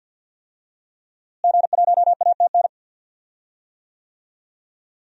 Morse code, heard as a keyed tone, about 51 wpm, 700 Hz. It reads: G1ATN
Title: DXCC-100 Call Signs - 5 Characters - Repeated 3x - 50wpm